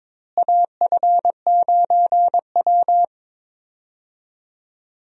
Morse code, heard as a keyed tone, about 22 wpm, 700 Hz. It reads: AF9W
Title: USA Call Signs - 4 Characters - Repeated 3x - 22wpm